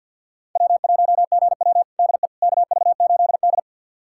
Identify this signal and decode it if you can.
Morse code, heard as a keyed tone, about 49 wpm, 690 Hz. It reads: W1GW DE KV8D
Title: USA Callsign DE Callsign - 4 Characters - 50wpm